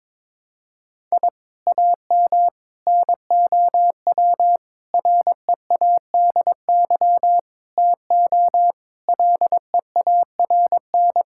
Morse code, heard as a keyed tone, about 22 wpm, 700 Hz. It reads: I AM NOW READY TO LEARN
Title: Sentences from Top 1000 Words - Review Part 6 - 22wpm